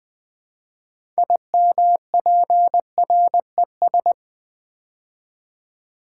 Morse code, heard as a keyed tone, about 20 wpm, 700 Hz. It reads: IMPRES
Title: Build A Word - 3 Letters At A Time - 9-Letter Words - 20wpm